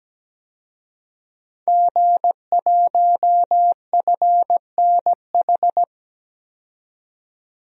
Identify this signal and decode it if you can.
Morse code, heard as a keyed tone, about 17 wpm, 700 Hz. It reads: G1FNH
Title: DXCC-100 Call Signs - 5 Characters - 17wpm